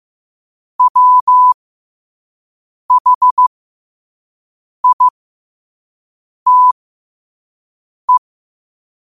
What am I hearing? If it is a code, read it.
Morse code, heard as a keyed tone, about 15 wpm, spaced as for 5 wpm, 1000 Hz. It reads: WHITE
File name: qotdfiles / 5wpm